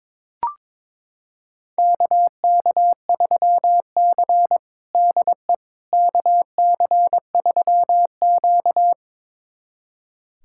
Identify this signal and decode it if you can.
Morse code, heard as a keyed tone, about 22 wpm, 700 Hz. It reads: KK3C DE KC3Q
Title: USA Callsign DE Callsign - 4 Characters - 22wpm